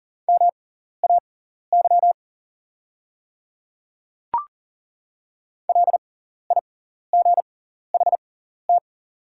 Morse code, heard as a keyed tone, about 40 wpm, spaced as for 14 wpm, 700 Hz. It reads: MAY LIGHT
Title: 6x Character Spacing - Top 500 Words - 40wpm